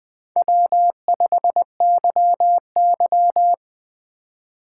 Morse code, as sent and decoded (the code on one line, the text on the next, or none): .-- ..... -.-- -.--
W5YY